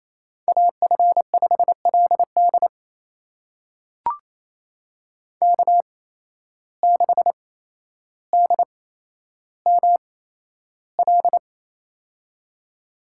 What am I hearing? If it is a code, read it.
Morse code, heard as a keyed tone, about 28 wpm, spaced as for 12 wpm, 700 Hz. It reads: AF5LD K6DML
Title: USA Call Signs - 5 Characters - 8x Character Spacing - 28wpm